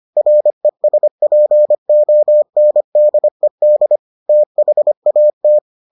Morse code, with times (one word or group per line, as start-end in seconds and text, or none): 0.2-4.0 RESPONDED
4.3-5.6 THAT